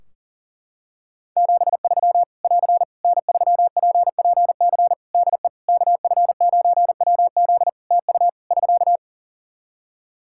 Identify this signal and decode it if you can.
Morse code, heard as a keyed tone, about 40 wpm, 700 Hz. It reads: 73 <AR> N3PPC DE KF9WZ TU <SK>
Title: K1USN SST QSO Practice 40wpm